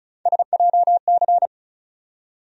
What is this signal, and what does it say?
Morse code, heard as a keyed tone, about 36 wpm, 700 Hz. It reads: SJC
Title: USA Call Sign Postfixes - 3 Letters 35wpm